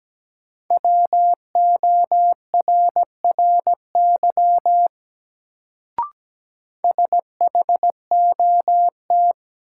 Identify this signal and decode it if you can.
Morse code, heard as a keyed tone, about 17 wpm, 700 Hz. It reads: WORRY SHOT
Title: Sentences from Top 1100 Words - New Word Drill - 17wpm